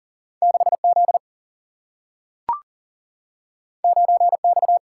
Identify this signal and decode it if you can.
Morse code, heard as a keyed tone, about 39 wpm, 700 Hz. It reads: 6Z 9X
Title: Single-Letter Single-Number 40wpm